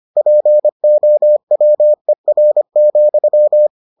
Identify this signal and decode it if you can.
Morse code, heard as a keyed tone, about 25 wpm, 600 Hz. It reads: POWER,